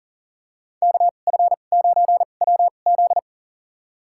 Morse code, heard as a keyed tone, about 40 wpm, spaced as for 29 wpm, 700 Hz. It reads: KF9WZ
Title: USA Call Signs - 2x Character Spacing - 40wpm